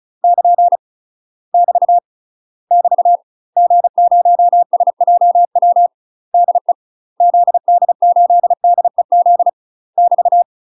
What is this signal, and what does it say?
Morse code, heard as a keyed tone, about 35 wpm, 700 Hz. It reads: <KN> <BT> <BT> G0SJW DE ZD8DEZ <BT>
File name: UK MASTER QSOs TRAINING/35WPM